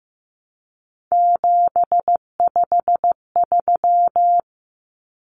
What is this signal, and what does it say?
Morse code, heard as a keyed tone, about 15 wpm, 700 Hz. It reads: 753
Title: Common Cut Numbers - 3-Digit 15wpm